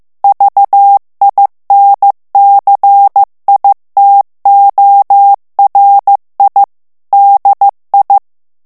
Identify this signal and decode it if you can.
Morse code, heard as a keyed tone, about 15 wpm, 800 Hz. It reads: VINCITORI DI
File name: morse mp3 / ITALIEN texte clair 2 vitesse 14 mots minute